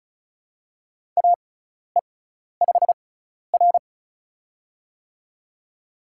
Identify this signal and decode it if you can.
Morse code, heard as a keyed tone, about 36 wpm, spaced as for 12 wpm, 700 Hz. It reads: AE5R
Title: USA Call Signs - 4 Characters - 6x Character Spacing - 35wpm